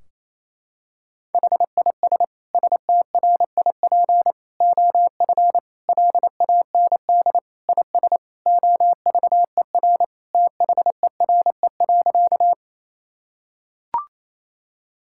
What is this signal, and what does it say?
Morse code, heard as a keyed tone, about 28 wpm, 700 Hz. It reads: HIS STRIP OF LAND IS OVER THERE.
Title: Sentences from Top 1600 Words 28wpm